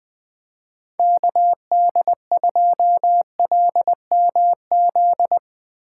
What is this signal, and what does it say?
Morse code, heard as a keyed tone, about 21 wpm, 700 Hz. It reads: KD2LMZ
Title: Speed Racing - USA Call Signs 20wpm